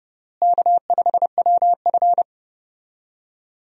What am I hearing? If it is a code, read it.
Morse code, heard as a keyed tone, about 30 wpm, 700 Hz. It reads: K5WF
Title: USA Call Signs - 4 Characters - Repeated 3x - 30wpm